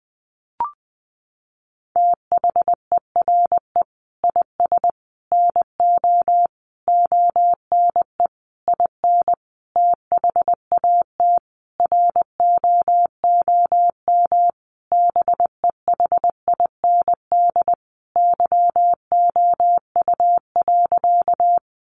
Morse code, heard as a keyed tone, about 20 wpm, 700 Hz. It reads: THERE IS NO ONE IN THAT ROOM BEHIND YOU.
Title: Sentences from Top 500 Words 20wpm